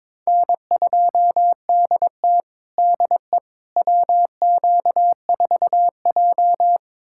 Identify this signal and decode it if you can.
Morse code, heard as a keyed tone, about 22 wpm, 700 Hz. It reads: N2DT DE WQ4J
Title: USA Callsign DE Callsign - 4 Characters - 22wpm